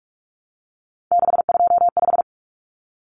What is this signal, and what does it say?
Morse code, heard as a keyed tone, about 44 wpm, 700 Hz. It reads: 625
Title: Numbers - 3-Digit 45wpm